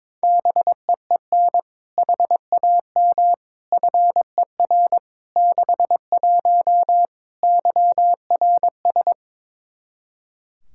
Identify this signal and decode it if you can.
Morse code, heard as a keyed tone, about 22 wpm, 700 Hz. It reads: BEEN HAM FER 61 YRS
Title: K1USN SST QSO Practice 22wpm